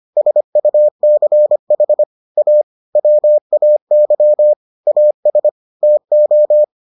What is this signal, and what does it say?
Morse code, heard as a keyed tone, about 25 wpm, 600 Hz. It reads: SUCH A WAY AS TO